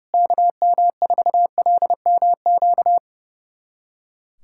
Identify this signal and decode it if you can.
Morse code, heard as a keyed tone, about 31 wpm, 700 Hz. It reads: KM4LMQ
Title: USA Callsign DE Callsign 30wpm